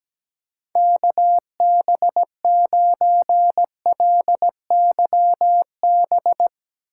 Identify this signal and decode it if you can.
Morse code, heard as a keyed tone, about 17 wpm, 700 Hz. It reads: KB9LYB
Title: USA Call Signs - 4x Character Spacing - 17wpm